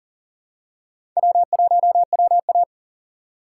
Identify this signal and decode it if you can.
Morse code, heard as a keyed tone, about 40 wpm, 700 Hz. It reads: W1WA
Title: USA Call Signs - 4 Characters - 3x Character Spacing - 40wpm